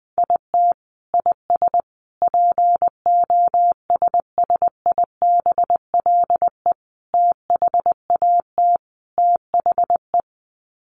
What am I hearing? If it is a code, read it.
Morse code, heard as a keyed tone, about 20 wpm, 700 Hz. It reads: IT IS POSSIBLE THAT THE
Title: Sets of 5 Words Encore2 20wpm